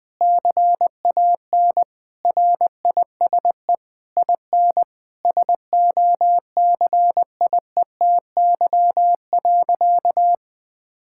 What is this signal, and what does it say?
Morse code, heard as a keyed tone, about 20 wpm, 700 Hz. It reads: CAN RISE IN SOCIETY.
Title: Sentences from Top 1000 Words - Review Part 6 - 20wpm